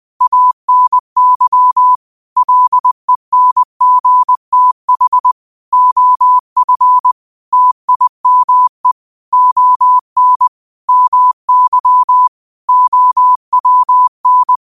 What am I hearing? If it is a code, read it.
Morse code, heard as a keyed tone, about 20 wpm, 1000 Hz. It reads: ANY LENGTH OF TIME ON MY OWN
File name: qotdfiles / 20wpm